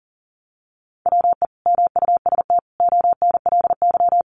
Morse code, heard as a keyed tone, about 40 wpm, 700 Hz. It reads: WE MUST ONLY
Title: Sentences from Top 1000 Words 40wpm